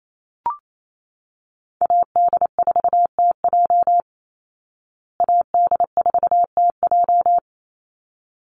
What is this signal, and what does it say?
Morse code, heard as a keyed tone, about 28 wpm, 700 Hz. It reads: AD4TJ AD4TJ
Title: Call Signs Repeated 3x - 28wpm